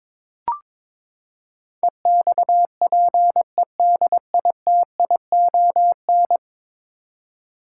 Morse code, heard as a keyed tone, about 21 wpm, 700 Hz. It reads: EXPEDITION
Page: morse-code-ninja-practice-22wpm